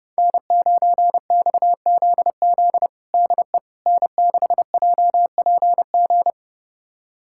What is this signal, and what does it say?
Morse code, heard as a keyed tone, about 32 wpm, 700 Hz. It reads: N9XZZ DE N6JPG
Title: USA Callsign DE Callsign - 5 Characters - 30wpm